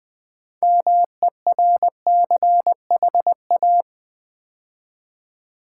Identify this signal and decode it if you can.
Morse code, heard as a keyed tone, about 20 wpm, 700 Hz. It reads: MERCHA
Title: Build A Word - 2 Letters At A Time - 8-Letter Words - 20wpm